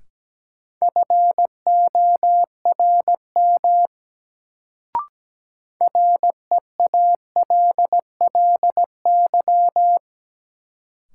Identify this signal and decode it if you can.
Morse code, heard as a keyed tone, about 17 wpm, 700 Hz. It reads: FORM REALLY
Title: Sets of 4 Words - New Word Drill - 17wpm